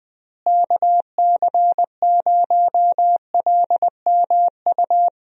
Morse code, heard as a keyed tone, about 20 wpm, 700 Hz. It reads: KC0LMU
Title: USA Call Signs - 6 Characters - Repeated 3x - 20wpm